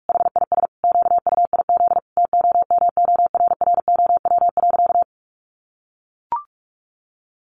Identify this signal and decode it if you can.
Morse code, heard as a keyed tone, about 44 wpm, 700 Hz. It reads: HIS QUIZ TOMORROW.
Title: Additional Easy Phrases 45wpm